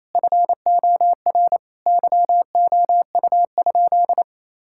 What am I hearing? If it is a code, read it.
Morse code, heard as a keyed tone, about 28 wpm, 700 Hz. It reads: FOR YOU?
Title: Sentences from Top 1000 Words - Review Part 6 - 28wpm